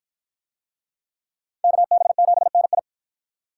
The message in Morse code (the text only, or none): KB7NI